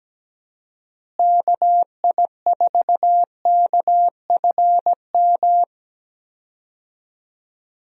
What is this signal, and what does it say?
Morse code, heard as a keyed tone, about 17 wpm, 700 Hz. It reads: KI4KFM
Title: USA Call Signs - 6 Characters - Repeated 3x - 17wpm